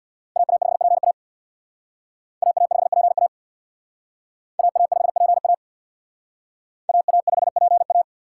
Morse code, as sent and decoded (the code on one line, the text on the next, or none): .- .- ..... .--. .- / .- .- ..... .--. .- / .- .- ..... .--. .- / .- .- ..... .--. .-
AA5PA AA5PA AA5PA AA5PA